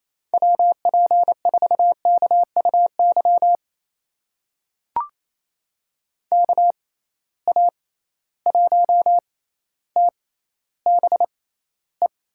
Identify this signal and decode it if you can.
Morse code, heard as a keyed tone, about 28 wpm, spaced as for 14 wpm, 700 Hz. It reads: WP4KUY KA1TBE
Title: USA Call Signs - 6 Characters - 6x Character Spacing - 28wpm